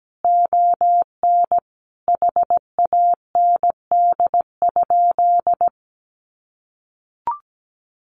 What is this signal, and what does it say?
Morse code, heard as a keyed tone, about 17 wpm, 700 Hz. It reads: ON HAND?
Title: Sentences from Top 1500 Words 17wpm